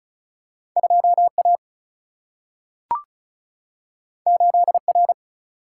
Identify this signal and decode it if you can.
Morse code, heard as a keyed tone, about 36 wpm, 700 Hz. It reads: 2 A 8 R
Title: Single-Letter Single-Number 35wpm